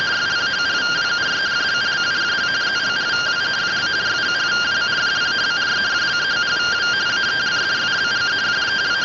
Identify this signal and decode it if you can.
Two-tone teleprinter signal, about 170 Hz shift.